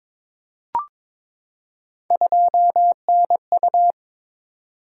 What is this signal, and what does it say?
Morse code, heard as a keyed tone, about 22 wpm, 700 Hz. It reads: 2NU
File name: practice / 22wpm / Three Letter-Number Combinations - 22wpm